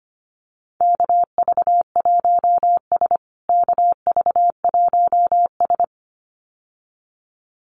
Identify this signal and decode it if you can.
Morse code, heard as a keyed tone, about 25 wpm, 700 Hz. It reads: KV1S KV1S
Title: USA Call Signs Repeated 2x 25wpm